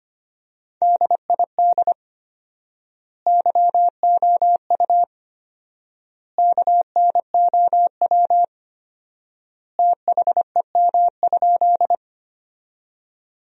Morse code, heard as a keyed tone, about 25 wpm, 700 Hz. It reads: DID YOU KNOW THEM?
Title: 4x Word-Spacing - Sentences from Top 100 Words - 25wpm